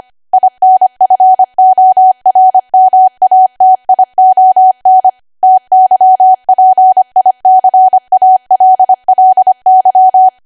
Morse code, heard as a keyed tone, about 25 wpm, 750 Hz. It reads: INFORMATION TYPICALLY
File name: Archive / 25 WPM / 25WPM